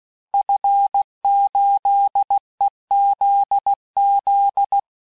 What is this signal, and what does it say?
Morse code, heard as a keyed tone, about 16 wpm, 800 Hz. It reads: F8EZZ
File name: cours-du-lundi files / cours-1580-fin-QSO